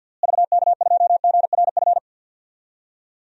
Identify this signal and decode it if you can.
Morse code, heard as a keyed tone, about 52 wpm, 700 Hz. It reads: VK2GRF
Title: DXCC-100 Call Signs - 6 Characters - Repeated 3x - 50wpm